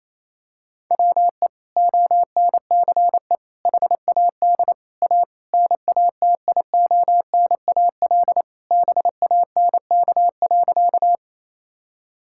Morse code, heard as a keyed tone, about 28 wpm, 700 Hz. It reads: WE ONCE HAD A NATIONAL BANK.
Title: Sentences from Top 1000 Words - Review Part 6 - 28wpm